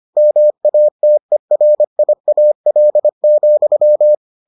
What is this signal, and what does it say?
Morse code, heard as a keyed tone, about 25 wpm, 600 Hz. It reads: MATERIAL,